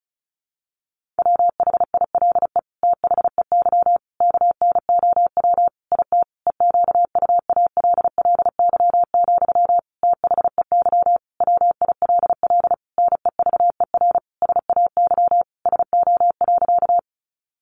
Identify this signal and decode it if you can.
Morse code, heard as a keyed tone, about 34 wpm, 700 Hz. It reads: WHILE THEY KNOW IT EQUALLY, THEY WILL NEVER SAY SO.
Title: Sentences from Top 1400 Words 35wpm